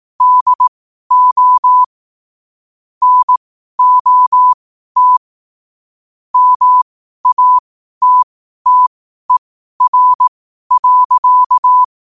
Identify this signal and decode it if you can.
Morse code, heard as a keyed tone, about 18 wpm, spaced as for 12 wpm, 1000 Hz. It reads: DO NOT MATTER.
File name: qotdfiles / 12wpm